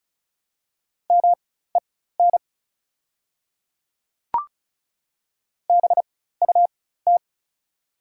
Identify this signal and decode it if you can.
Morse code, heard as a keyed tone, about 36 wpm, spaced as for 16 wpm, 700 Hz. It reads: MEN BUT
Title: 4x Character Spacing - Top 200 Words - 35wpm